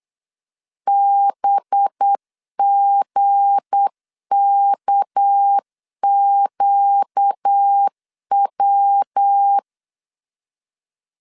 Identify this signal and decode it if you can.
Morse code, heard as a keyed tone, about 8 wpm, 800 Hz. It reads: BGKQW
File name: LEZIONI / 7 GF9Z / 7 GF 8wpm 3108